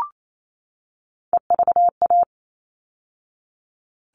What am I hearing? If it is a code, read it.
Morse code, heard as a keyed tone, about 28 wpm, 700 Hz. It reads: EVA
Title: All Cut Numbers - 3-Digit 28wpm